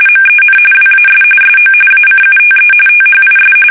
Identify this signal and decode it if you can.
RTTY, about 45.45 baud. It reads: MEN WAS SO ACUTE THIS
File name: fsk850 / usnavy2